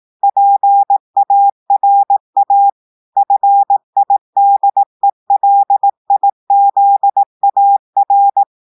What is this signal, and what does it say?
Morse code, heard as a keyed tone, about 18 wpm, 800 Hz. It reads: PARA FIDELIZAR
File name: audios / portugais / portugais texte en clair 2 vitesse 18 mots minutes